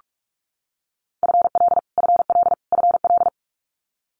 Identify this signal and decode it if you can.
Morse code, heard as a keyed tone, about 44 wpm, 700 Hz. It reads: FL FL FL
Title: US States Repeated 3x 45wpm